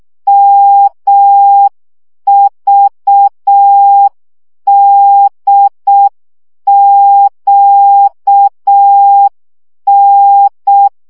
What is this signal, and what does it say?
Morse code, heard as a keyed tone, about 6 wpm, 800 Hz. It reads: MVDQN
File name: morze vizsga 2011-1 5